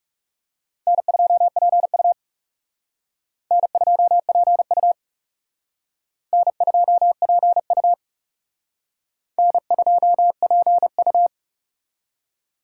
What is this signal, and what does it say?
Morse code, heard as a keyed tone, about 36 wpm, 700 Hz. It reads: N2PU N2PU N2PU N2PU
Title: Speed Racing - USA Call Signs 30wpm